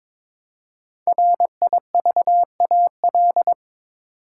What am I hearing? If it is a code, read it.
Morse code, heard as a keyed tone, about 22 wpm, 700 Hz. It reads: RIVAL
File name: 5 Letter Words - First 3 Letters Given - 22wpm